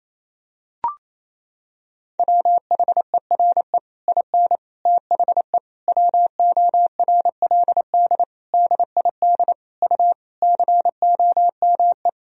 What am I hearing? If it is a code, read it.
Morse code, heard as a keyed tone, about 27 wpm, 700 Hz. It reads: WHERE IN THE WORLD DID U COME
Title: Sentences from Top 200 Words and Abbreviations 28wpm